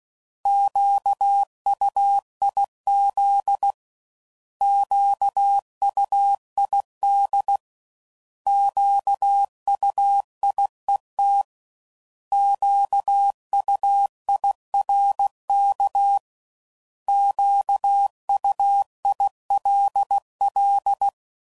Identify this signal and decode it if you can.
Morse code, heard as a keyed tone, about 16 wpm, 800 Hz. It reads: QUIZ QUID QUIET QUIRK QUILL
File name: Prefixes / By First Letter / Qs